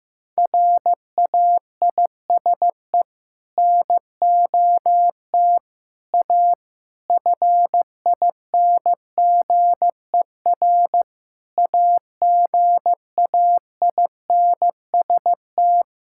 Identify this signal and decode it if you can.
Morse code, heard as a keyed tone, about 16 wpm, 700 Hz. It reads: RAISE NOT A FINGER AGAINST
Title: Sentences from Top 1600 Words 15wpm